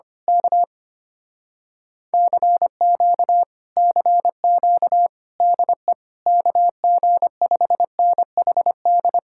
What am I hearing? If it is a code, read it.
Morse code, heard as a keyed tone, about 24 wpm, 700 Hz. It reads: K CQ CQ DE KG5NHD
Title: Speed-Racing Common QSO Phrases - 25wpm